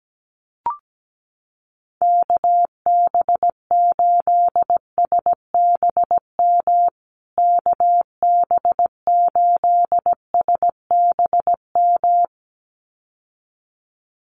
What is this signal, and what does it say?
Morse code, heard as a keyed tone, about 17 wpm, 700 Hz. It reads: KB8SBM KB8SBM
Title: USA Call Signs Repeated 2x 17wpm